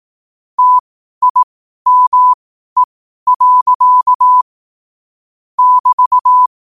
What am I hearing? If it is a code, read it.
Morse code, heard as a keyed tone, about 18 wpm, spaced as for 12 wpm, 1000 Hz. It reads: TIME. <BT>
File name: qotdfiles / 12wpm